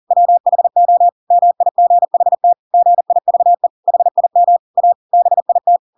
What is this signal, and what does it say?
Morse code, heard as a keyed tone, about 40 wpm, 700 Hz. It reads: WHO MIGHT GIVE HIM A BIT
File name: Pinocchio / Pinocchio ch5 40 40 ik1hge